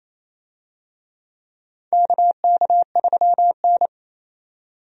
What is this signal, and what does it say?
Morse code, heard as a keyed tone, about 28 wpm, 700 Hz. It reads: KK3N
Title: USA Call Signs - 4 Characters - 28wpm